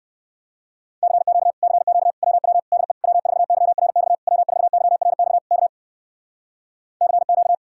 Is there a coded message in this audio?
Yes: Morse code from 1.0 to 7.5 s, heard as a keyed tone, about 59 wpm, 700 Hz. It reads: CQ CQ WW DE W4YRX W4YRX K CQ